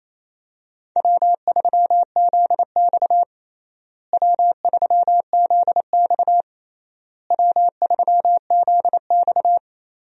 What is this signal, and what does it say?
Morse code, heard as a keyed tone, about 28 wpm, 700 Hz. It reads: W3ZX W3ZX W3ZX
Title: USA Call Signs - 4 Characters - Repeated 3x - 28wpm